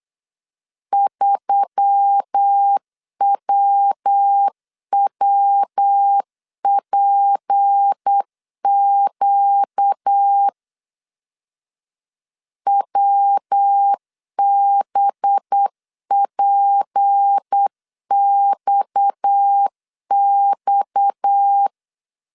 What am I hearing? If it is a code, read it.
Morse code, heard as a keyed tone, about 8 wpm, 800 Hz. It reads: 3WWPQ WBPXX
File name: CW / DET / 1 6 1500C S B 8wpm 5142